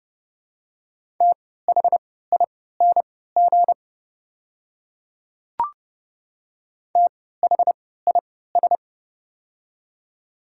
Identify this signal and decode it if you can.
Morse code, heard as a keyed tone, about 30 wpm, spaced as for 17 wpm, 700 Hz. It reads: THING THIS
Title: 3x Character Spacing - Top 100 Words - 30wpm